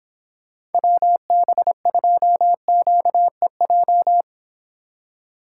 Morse code, heard as a keyed tone, about 26 wpm, 700 Hz. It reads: WB2QEJ
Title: Speed Racing - USA 2x3 Call Signs 22wpm